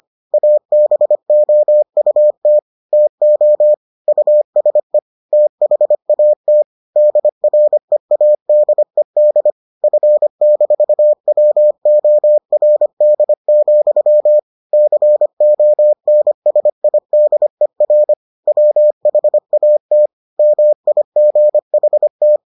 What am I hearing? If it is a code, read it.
Morse code, heard as a keyed tone, about 25 wpm, 600 Hz. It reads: ABOUT TO USE THAT DREADED F-WORD, CONSIDER WHAT MIGHT